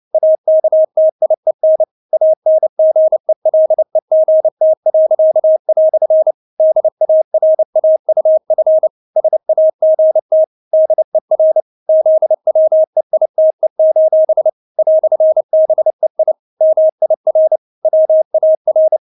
Morse code, heard as a keyed tone, about 29 wpm, 630 Hz. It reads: AKTIEN ANGELEGT." DARAUF SAGT DER ZWEITE: "BEI MIR WAR
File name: Witz10 29wpm